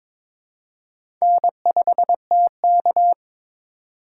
Morse code, heard as a keyed tone, about 21 wpm, 700 Hz. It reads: N5TK
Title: USA Call Signs - 4 Characters - 4x Character Spacing - 22wpm